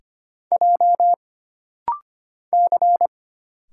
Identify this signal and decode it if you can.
Morse code, heard as a keyed tone, about 25 wpm, 700 Hz. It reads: J C
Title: Single Letters - Rapid-Fire 25wpm